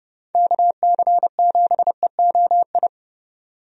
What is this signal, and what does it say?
Morse code, heard as a keyed tone, about 30 wpm, 700 Hz. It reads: KC7EOI
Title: USA Call Signs - 6 Characters - Repeated 2x - 30wpm